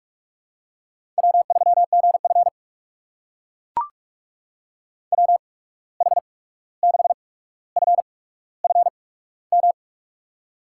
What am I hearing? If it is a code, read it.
Morse code, heard as a keyed tone, about 44 wpm, spaced as for 17 wpm, 700 Hz. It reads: W3GF WH6FFM
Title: USA Call Signs - 8x Character Spacing - 45wpm